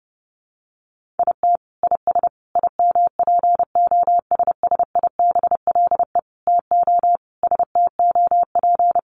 Morse code, heard as a keyed tone, about 30 wpm, 700 Hz. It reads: IT IS IMPOSSIBLE TO STOP
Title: Sentences from Top 1100 Words 30wpm